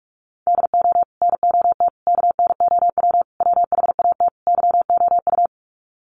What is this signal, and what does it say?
Morse code, heard as a keyed tone, about 44 wpm, 700 Hz. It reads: DO NOT KNOW WHAT YOU
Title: Sets of 5 Words 45wpm